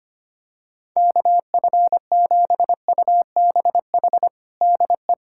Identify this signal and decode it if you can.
Morse code, heard as a keyed tone, about 25 wpm, 700 Hz. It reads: KF7UBH DE
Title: USA Callsign DE Callsign - 6 Characters - 25wpm